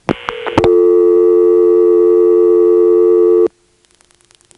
Dial tone (North America), 350 + 440 Hz.